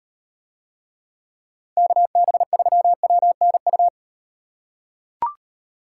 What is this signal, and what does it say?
Morse code, heard as a keyed tone, about 36 wpm, 700 Hz. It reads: KB3WNU